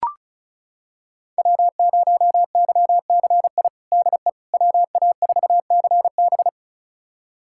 Morse code, heard as a keyed tone, about 36 wpm, 700 Hz. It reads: W0YCI DE WA4CB
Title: USA Callsign DE Callsign - 5 Characters - 35wpm